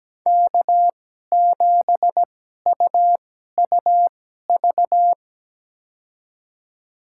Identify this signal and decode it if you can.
Morse code, heard as a keyed tone, about 17 wpm, spaced as for 12 wpm, 700 Hz. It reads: K7UUV
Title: USA Call Signs - 2x Character Spacing - 17wpm